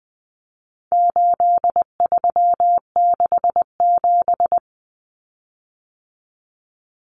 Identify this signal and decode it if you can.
Morse code, heard as a keyed tone, about 20 wpm, 700 Hz. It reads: 8367
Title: Common Cut Numbers - 4-Digit 20wpm